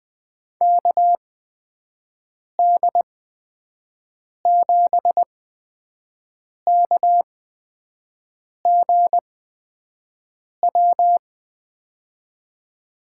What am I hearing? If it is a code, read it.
Morse code, heard as a keyed tone, about 20 wpm, spaced as for 5 wpm, 700 Hz. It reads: KD7KGW
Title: USA Call Signs - 6 Characters - 8x Character Spacing - 20wpm